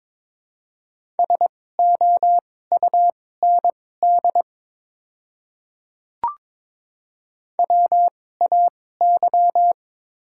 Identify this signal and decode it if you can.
Morse code, heard as a keyed tone, about 22 wpm, spaced as for 16 wpm, 700 Hz. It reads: SOUND WAY
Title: 2x Character Spacing - Top 100 Words - 22wpm